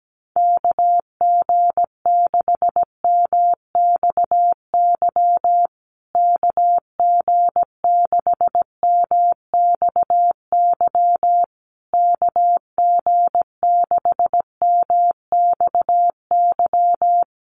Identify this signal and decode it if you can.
Morse code, heard as a keyed tone, about 17 wpm, 700 Hz. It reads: KG6MXY KG6MXY KG6MXY
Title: USA Call Signs Repeated 3x 17wpm